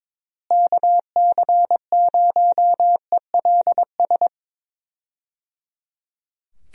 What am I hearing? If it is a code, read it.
Morse code, heard as a keyed tone, about 22 wpm, 700 Hz. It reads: KC0ELS
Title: USA Call Signs - 6 Characters - Repeated 3x - 22wpm